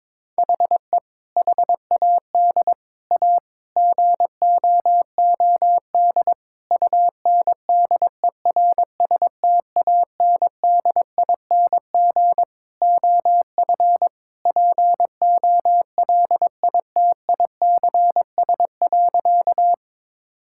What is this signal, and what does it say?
Morse code, heard as a keyed tone, about 22 wpm, 700 Hz. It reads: HE HAD A GOOD UNDERSTANDING OF POLITICS.
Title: Sentences from Top 1000 Words - Review Part 6 - 22wpm